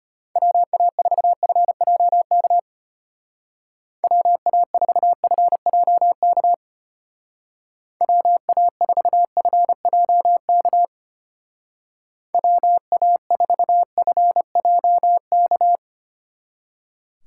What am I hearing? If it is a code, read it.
Morse code, heard as a keyed tone, about 32 wpm, 700 Hz. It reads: WA4FJK WA4FJK WA4FJK WA4FJK